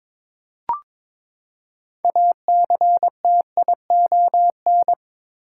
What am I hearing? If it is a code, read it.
Morse code, heard as a keyed tone, about 22 wpm, 700 Hz. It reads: ACTION
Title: Six Letter Words - 22wpm